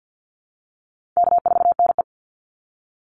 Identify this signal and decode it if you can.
Morse code, heard as a keyed tone, about 50 wpm, 700 Hz. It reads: K4NE
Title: Easy Call Signs Repeated 3x - 50wpm